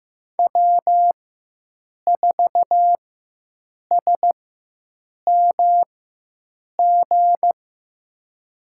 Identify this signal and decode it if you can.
Morse code, heard as a keyed tone, about 15 wpm, spaced as for 7 wpm, 700 Hz. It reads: W4SMG
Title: USA Call Signs - 5 Characters - 4x Character Spacing - 15wpm